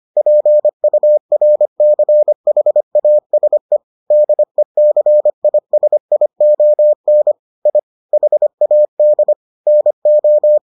Morse code, heard as a keyed tone, about 25 wpm, 600 Hz. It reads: PURCHASE DECISION I HAD NO